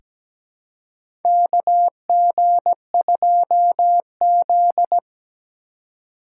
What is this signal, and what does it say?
Morse code, heard as a keyed tone, about 17 wpm, 700 Hz. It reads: KG2Z
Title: USA Call Signs - 2x Character Spacing - 17wpm